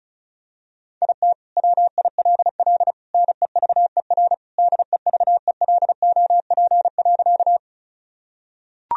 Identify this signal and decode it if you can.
Morse code, heard as a keyed tone, about 36 wpm, 700 Hz. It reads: IT WILL NEVER DEVELOP.
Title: Sentences from Top 1000 Words - Review Part 2 - 35wpm